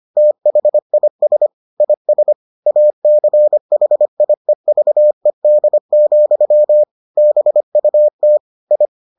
Morse code, heard as a keyed tone, about 25 wpm, 600 Hz. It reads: THIS IS ACHIEVED, BUT I